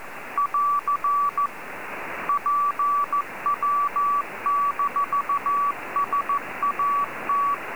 Morse code, heard as a keyed tone, about 14 wpm, 1140 Hz, with noise. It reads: <AR> PW-SAT